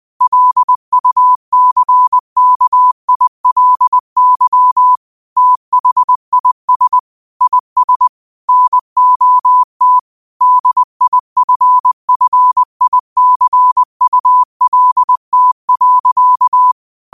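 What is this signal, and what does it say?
Morse code, heard as a keyed tone, about 20 wpm, 1000 Hz. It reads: LUCKILY THIS IS NOT DIFFICULT.
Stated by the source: Morse quote practice at 20 wpm.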